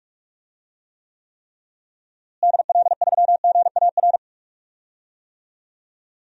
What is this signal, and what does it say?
Morse code, heard as a keyed tone, about 41 wpm, 700 Hz. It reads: DL3GAR
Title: DXCC-100 Call Signs - 6 Characters - Repeated 3x - 45wpm